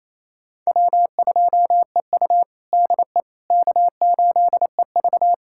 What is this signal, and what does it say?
Morse code, heard as a keyed tone, about 27 wpm, 700 Hz. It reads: W2EU DE K8EV
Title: USA Callsign DE Callsign - 4 Characters - 28wpm